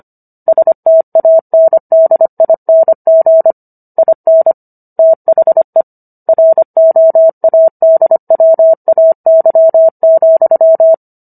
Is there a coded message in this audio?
Yes: Morse code from 0.5 to 10.9 s, heard as a keyed tone, about 25 wpm, 650 Hz. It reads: STANDING IN THE ROADWAY,